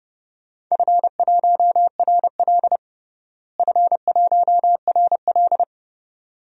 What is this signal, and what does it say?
Morse code, heard as a keyed tone, about 31 wpm, 700 Hz. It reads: F1RL F1RL
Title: DXCC-100 Call Signs - 4 Characters - Repeated 2x - 30wpm